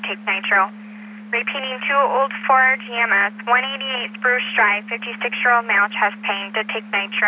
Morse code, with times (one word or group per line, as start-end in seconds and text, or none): none